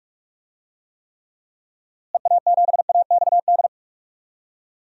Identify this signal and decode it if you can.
Morse code, heard as a keyed tone, about 46 wpm, 700 Hz. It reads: EA7AXD
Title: DXCC-100 Call Signs - 6 Characters - Repeated 3x - 45wpm